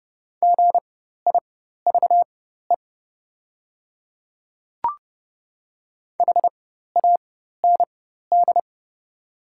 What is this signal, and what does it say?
Morse code, heard as a keyed tone, about 30 wpm, spaced as for 14 wpm, 690 Hz. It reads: GIVE HAND
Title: 4x Character Spacing - Top 200 Words - 30wpm